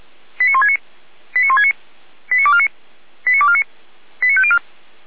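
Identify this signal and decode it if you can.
Five-tone selective call (ZVEI), heard as quick runs of five tones, tones of about 80 ms. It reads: 89169 89169 89249 89249 89504
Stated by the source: ZVEI five-tone call received by SDR, from the feuerwehr-orlamuende.org fire brigade audio archive.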